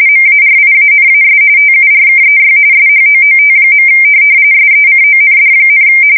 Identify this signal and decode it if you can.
RTTY, about 45.45 baud. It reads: MARIANAS TURKEY SHOOT. SECOND,
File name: fsk170 / USS-New-Jersey